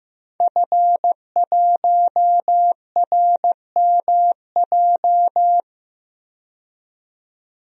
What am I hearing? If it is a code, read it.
Morse code, heard as a keyed tone, about 15 wpm, 700 Hz. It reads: F1RMJ
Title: DXCC-100 Call Signs - 5 Characters - 15wpm